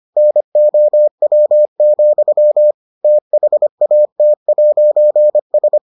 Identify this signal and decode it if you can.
Morse code, heard as a keyed tone, about 25 wpm, 600 Hz. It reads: NOW, THAT'S